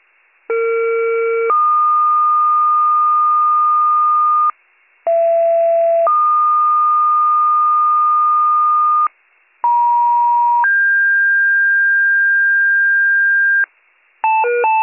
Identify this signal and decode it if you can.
Two-tone page, heard as a one-second tone, then a three-second tone, about 460 then 1220 Hz, about 670 then 1220 Hz, about 950 then 1600 Hz. It is heard from 0.5 to 13.7 s.